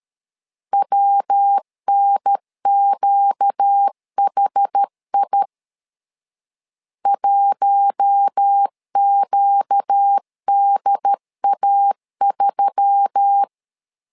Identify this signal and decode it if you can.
Morse code, heard as a keyed tone, about 13 wpm, 800 Hz. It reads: WNQHI 1QDA3
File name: LEZIONI / 4 Q1S3 12wpm 2023